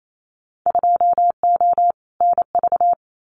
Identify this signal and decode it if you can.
Morse code, heard as a keyed tone, about 28 wpm, 700 Hz. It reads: 2O NV
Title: Winter Field Day Exchanges 28wpm